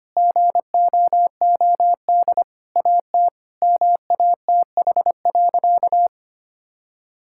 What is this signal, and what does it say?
Morse code, heard as a keyed tone, about 25 wpm, 700 Hz. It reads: GOOD AT MATH.
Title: Sentences from Top 1000 Words - Review Part 6 - 25wpm